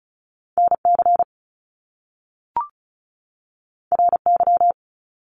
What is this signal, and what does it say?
Morse code, heard as a keyed tone, about 35 wpm, 700 Hz. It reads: NC RY
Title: Top Two Letter Combinations in Words 35wpm